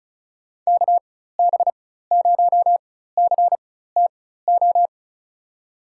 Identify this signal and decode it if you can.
Morse code, heard as a keyed tone, about 36 wpm, spaced as for 16 wpm, 700 Hz. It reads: KB0CTO
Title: USA Call Signs - 6 Characters - 4x Character Spacing - 35wpm